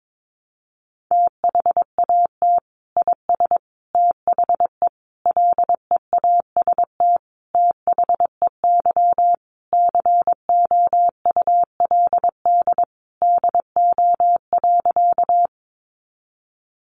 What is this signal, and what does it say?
Morse code, heard as a keyed tone, about 22 wpm, 700 Hz. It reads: THAT IS THE LEAST THEY COULD DO.
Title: Sentences from Top 700 Words 22wpm